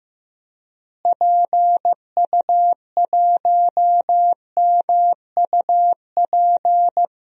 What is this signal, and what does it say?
Morse code, heard as a keyed tone, about 15 wpm, 700 Hz. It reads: PU1MUP
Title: DXCC-100 Call Signs - 6 Characters - Repeated 2x - 15wpm